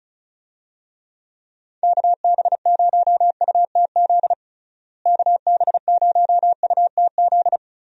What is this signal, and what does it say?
Morse code, heard as a keyed tone, about 36 wpm, 700 Hz. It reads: KB0UTZ KB0UTZ
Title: USA Call Signs - 6 Characters - Repeated 3x - 35wpm